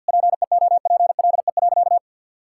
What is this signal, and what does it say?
Morse code, heard as a keyed tone, about 50 wpm, 700 Hz. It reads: PEOPLE.
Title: Sentences from Top 1000 Words - Review Part 5 - 50wpm